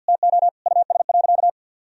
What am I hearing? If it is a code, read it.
Morse code, heard as a keyed tone, about 50 wpm, 700 Hz. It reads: TO US.
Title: Sentences from Top 1000 Words - Review Part 4 - 50wpm